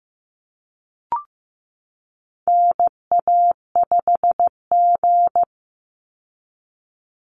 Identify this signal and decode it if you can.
Morse code, heard as a keyed tone, about 15 wpm, 700 Hz. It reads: NA5G
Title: Call Signs Repeated 3x - 15wpm